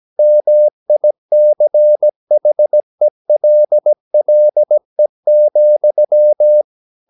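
Morse code, heard as a keyed tone, about 17 wpm, 600 Hz. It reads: MICHELLE,
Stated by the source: Slow (QRS) Morse practice at 17 wpm, from CW med Gnister.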